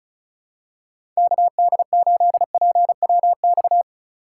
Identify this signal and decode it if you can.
Morse code, heard as a keyed tone, about 36 wpm, 700 Hz. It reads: KD8PWX
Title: USA Call Signs - 3x Character Spacing - 35wpm